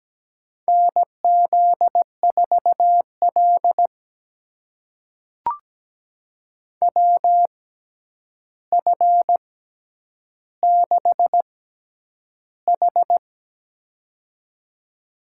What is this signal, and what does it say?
Morse code, heard as a keyed tone, about 17 wpm, spaced as for 9 wpm, 700 Hz. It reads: NZ4L WF6H
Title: USA Call Signs - 6x Character Spacing - 17wpm